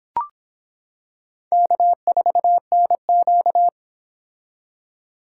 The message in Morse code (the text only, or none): K4NQ